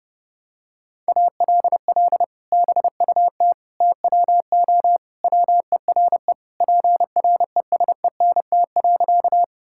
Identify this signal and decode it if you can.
Morse code, heard as a keyed tone, about 30 wpm, 700 Hz. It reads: ALL BUT TWO WERE PRESENT.
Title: Sentences from Top 1000 Words - Review Part 5 - 30wpm